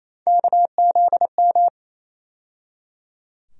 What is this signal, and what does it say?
Morse code, heard as a keyed tone, about 28 wpm, 700 Hz. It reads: KZM
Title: USA Call Sign Postfixes - 3 Letters 28wpm